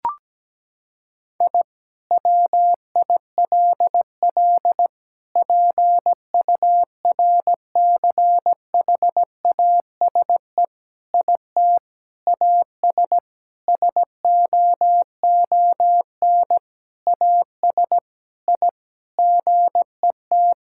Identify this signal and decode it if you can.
Morse code, heard as a keyed tone, about 17 wpm, 700 Hz. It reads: I WILL PURCHASE IT AS SOON AS I GET
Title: Sentences from Top 1000 Words - Review Part 5 - 17wpm